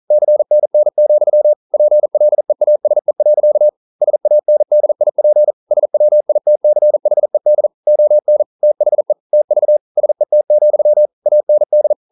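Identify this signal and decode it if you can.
Morse code, heard as a keyed tone, about 42 wpm, 600 Hz. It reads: CNN, PLEASE. SANDIP SWITCHED ON THE TV SET, AND
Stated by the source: Morse code practice at 41 wpm, from CW med Gnister.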